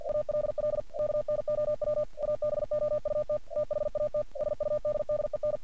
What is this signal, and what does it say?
Morse code, heard as a keyed tone, about 49 wpm, 600 Hz. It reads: ALL KNOW ABOUT THAT SUDDEN
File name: archived / CRSnet